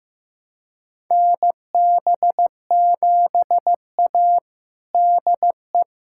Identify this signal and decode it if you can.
Morse code, heard as a keyed tone, about 15 wpm, 700 Hz. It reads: NB7A DE
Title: USA Callsign DE Callsign - 4 Characters - 15wpm